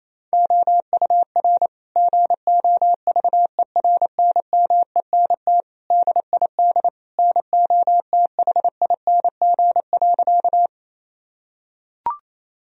Morse code, heard as a keyed tone, about 28 wpm, 700 Hz. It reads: OUR GOVERNMENT DID NOTHING.
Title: Sentences from Top 1000 Words - Review Part 3 - 28wpm